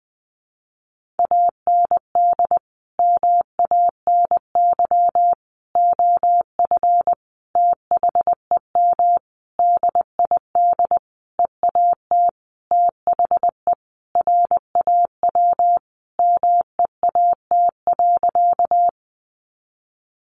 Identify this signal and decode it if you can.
Morse code, heard as a keyed tone, about 20 wpm, 700 Hz. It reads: AND MANY OF THEM DID EAT THE RAW MEAT.
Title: Sentences from Top 1500 Words 20wpm